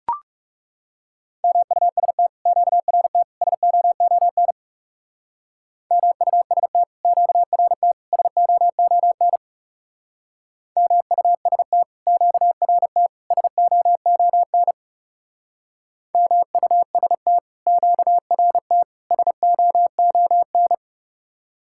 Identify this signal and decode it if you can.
Morse code, heard as a keyed tone, about 37 wpm, 700 Hz. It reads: MUST QRT SOON MUST QRT SOON MUST QRT SOON MUST QRT SOON
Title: Speed-Racing Common QSO Phrases Encore - 30wpm